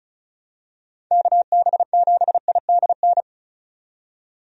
Morse code, heard as a keyed tone, about 36 wpm, 700 Hz. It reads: KB7IDN
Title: USA Call Signs - 6 Characters - 35wpm